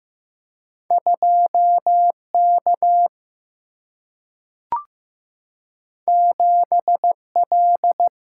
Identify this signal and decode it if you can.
Morse code, heard as a keyed tone, about 15 wpm, 700 Hz. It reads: 2 K 7 L
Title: Single-Letter Single-Number 15wpm